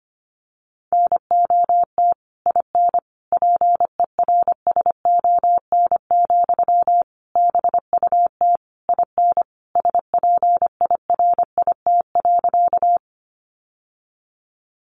Morse code, heard as a keyed tone, about 25 wpm, 700 Hz. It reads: NOT IN PERSON, BUT IN SPIRIT.
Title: Sentences from Top 1400 Words 25wpm